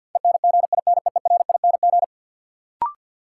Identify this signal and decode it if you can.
Morse code, heard as a keyed tone, about 52 wpm, 700 Hz. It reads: ENGINEERING
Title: Sentences from Top 1500 Words - New Word Drill - 50wpm